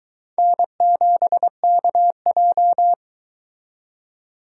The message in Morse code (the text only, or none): N7KJ